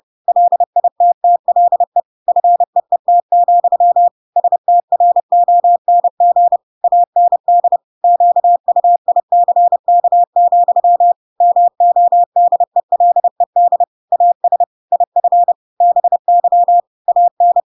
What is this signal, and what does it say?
Morse code, heard as a keyed tone, about 30 wpm, 700 Hz. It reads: LITTLE FEET, STRONG AND QUICK, MODELED AS IF BY AN
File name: Pinocchio / Pinocchio ch8 30 30 ik1hge